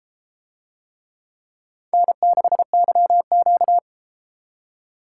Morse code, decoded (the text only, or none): N6YQ